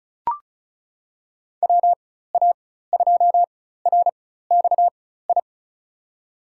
Morse code, heard as a keyed tone, about 36 wpm, spaced as for 16 wpm, 700 Hz. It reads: WA2RXI
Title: USA Call Signs - 6 Characters - 4x Character Spacing - 35wpm